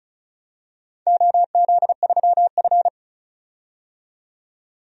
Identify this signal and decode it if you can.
Morse code, heard as a keyed tone, about 34 wpm, 700 Hz. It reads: OZ3F
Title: DXCC-100 Call Signs - 4 Characters - 35wpm